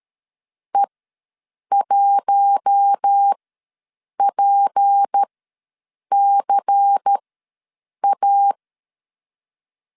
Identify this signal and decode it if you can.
Morse code, heard as a keyed tone, about 13 wpm, spaced as for 7 wpm, 800 Hz. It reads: E1PCA
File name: LEZIONI / 6 PKX0 / 6 P ws 12wpm 3117